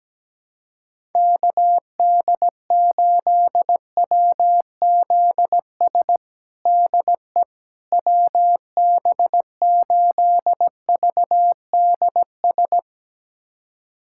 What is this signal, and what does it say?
Morse code, heard as a keyed tone, about 17 wpm, 700 Hz. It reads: KD8WZS DE WB8VDS
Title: USA Callsign DE Callsign - 6 Characters - 17wpm